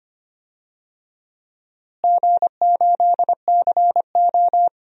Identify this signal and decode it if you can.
Morse code, heard as a keyed tone, about 25 wpm, 700 Hz. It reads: G8CO
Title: DXCC-100 Call Signs - 4 Characters - Repeated 2x - 25wpm